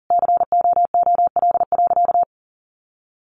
Morse code, heard as a keyed tone, about 40 wpm, 700 Hz. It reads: COOL.
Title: Sentences from Top 1200 Words 40wpm